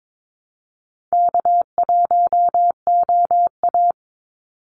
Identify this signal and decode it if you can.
Morse code, heard as a keyed tone, about 22 wpm, 700 Hz. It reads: K1OA
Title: Call Signs Repeated 2x - 22wpm